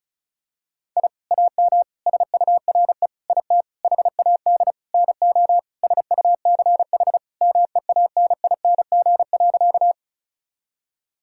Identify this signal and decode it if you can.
Morse code, heard as a keyed tone, about 34 wpm, 690 Hz. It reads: I AM SURE IT HAD NO SUCH MEANING.
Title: Sentences from Top 1000 Words - Review Part 6 - 35wpm